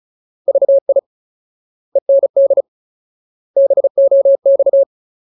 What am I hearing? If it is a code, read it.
Morse code, heard as a keyed tone, about 34 wpm, 550 Hz. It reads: VI END BOX
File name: Ultimate Word List / 35WPM Ultimate